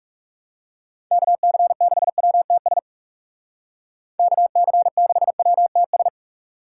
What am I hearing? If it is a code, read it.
Morse code, heard as a keyed tone, about 35 wpm, 700 Hz. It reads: KC6WTS KC6WTS
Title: Speed Racing - USA 2x3 Call Signs 30wpm